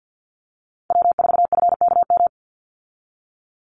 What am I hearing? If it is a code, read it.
Morse code, heard as a keyed tone, about 50 wpm, 700 Hz. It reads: W4FKM
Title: Call Signs Repeated 3x - 50wpm